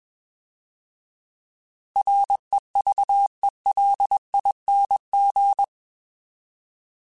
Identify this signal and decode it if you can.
Morse code, heard as a keyed tone, about 21 wpm, 800 Hz. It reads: REVELING